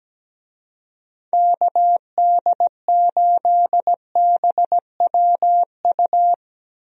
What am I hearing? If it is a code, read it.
Morse code, heard as a keyed tone, about 17 wpm, 700 Hz. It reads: KD8BWU
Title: USA Call Signs - 6 Characters - Repeated 2x - 17wpm